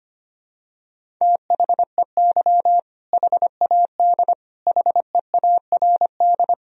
Morse code, heard as a keyed tone, about 25 wpm, 700 Hz. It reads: THEY HAD HEARD
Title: Sentences from Top 400 Words and Abbreviations 25wpm